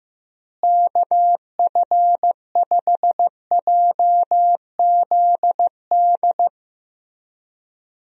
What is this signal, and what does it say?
Morse code, heard as a keyed tone, about 16 wpm, 700 Hz. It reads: KF5JZD
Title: USA Call Signs - 6 Characters - Repeated 2x - 15wpm